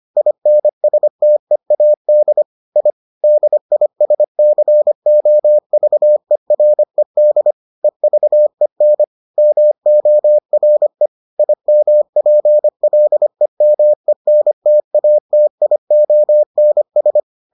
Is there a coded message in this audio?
Yes: Morse code from 0.2 to 17.2 s, heard as a keyed tone, about 25 wpm, 600 Hz. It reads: INSTEAD I DISCOVERED EVEN MORE IMPLEMENTATIONS